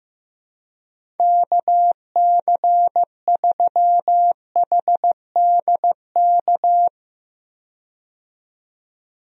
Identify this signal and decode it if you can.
Morse code, heard as a keyed tone, about 15 wpm, 700 Hz. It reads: KC3HDK
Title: USA Call Signs - 6 Characters - 15wpm